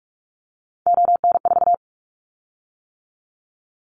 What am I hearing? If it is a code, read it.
Morse code, heard as a keyed tone, about 46 wpm, 700 Hz. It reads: ON4
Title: Callsign Prefixes for Top 50 DXCC - 45wpm